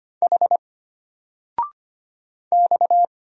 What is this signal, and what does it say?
Morse code, heard as a keyed tone, about 25 wpm, 700 Hz. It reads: H X
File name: Single-Letter-Number-25wpm